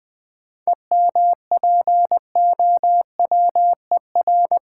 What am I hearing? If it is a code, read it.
Morse code, heard as a keyed tone, about 20 wpm, 700 Hz. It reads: EMPOWER
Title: 7 Letter Words - First 3 Letters Given - 20wpm